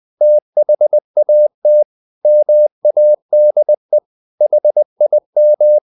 Morse code, heard as a keyed tone, about 20 wpm, 600 Hz. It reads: THAT MADE HIM
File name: the old man and the sea part 2 20wpm